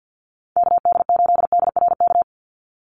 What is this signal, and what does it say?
Morse code, heard as a keyed tone, about 49 wpm, 700 Hz. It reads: KD8DRK
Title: USA Call Signs 50wpm